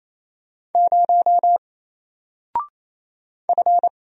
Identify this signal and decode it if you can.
Morse code, heard as a keyed tone, about 28 wpm, 700 Hz. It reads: 0 F
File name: Single-Letter-Number-28wpm